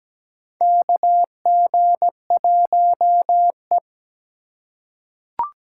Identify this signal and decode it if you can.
Morse code, heard as a keyed tone, about 17 wpm, 700 Hz. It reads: KG1E
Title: USA Call Signs - 3x Character Spacing - 17wpm